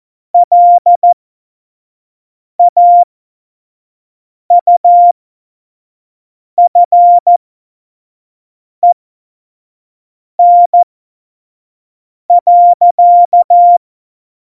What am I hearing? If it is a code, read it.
Morse code, heard as a keyed tone, about 14 wpm, spaced as for 5 wpm, 700 Hz. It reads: LAUFEN.
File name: Morsekurs / Witze / Witz 29. Dezember 2020, 25 F70